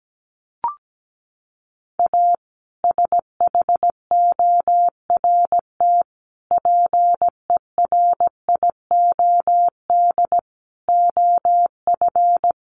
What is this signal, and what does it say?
Morse code, heard as a keyed tone, about 17 wpm, 700 Hz. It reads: A SHORT PERIOD OF
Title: Sets of 5 Words Encore 17wpm